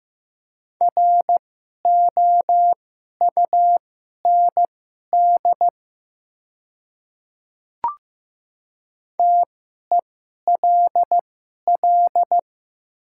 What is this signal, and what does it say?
Morse code, heard as a keyed tone, about 15 wpm, spaced as for 11 wpm, 700 Hz. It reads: ROUND TELL
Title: 2x Character Spacing - Top 200 Words - 15wpm